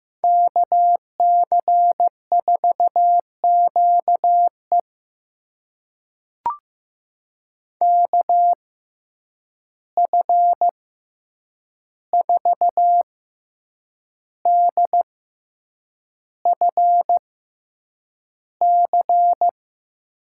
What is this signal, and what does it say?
Morse code, heard as a keyed tone, about 15 wpm, spaced as for 7 wpm, 700 Hz. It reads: KC4QE KF4DFC
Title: USA Call Signs - 6x Character Spacing - 15wpm